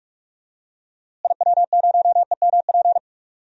Morse code, heard as a keyed tone, about 45 wpm, 700 Hz. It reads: IW0EMP
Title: DXCC-100 Call Signs - 6 Characters - Repeated 3x - 45wpm